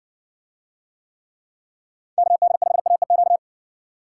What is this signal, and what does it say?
Morse code, heard as a keyed tone, about 60 wpm, 700 Hz. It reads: KD5AEQ